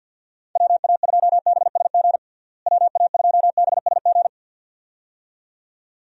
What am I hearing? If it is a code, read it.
Morse code, heard as a keyed tone, about 51 wpm, 700 Hz. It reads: WA2BSG WA2BSG
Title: USA Call Signs - 6 Characters - Repeated 2x - 50wpm